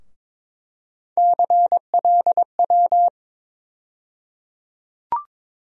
Morse code, heard as a keyed tone, about 22 wpm, 700 Hz. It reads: CLW
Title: USA Call Sign Postfixes - 3 Letters 22wpm